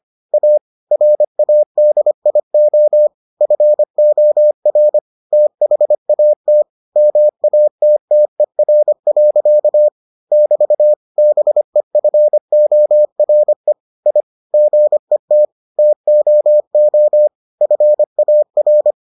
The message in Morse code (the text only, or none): A RADIO FOR THAT MATTER. <BT> BEFORE I GET TOO FAR